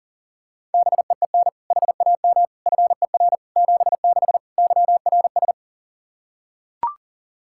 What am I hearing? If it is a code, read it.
Morse code, heard as a keyed tone, about 40 wpm, 700 Hz. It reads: BEEN HAM FER 76 YRS
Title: Practice QSO - Exchange 3 - More Info - 40wpm